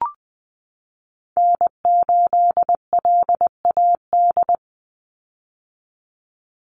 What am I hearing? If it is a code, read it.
Morse code, heard as a keyed tone, about 20 wpm, 700 Hz. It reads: N8LAD
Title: USA Call Signs 20wpm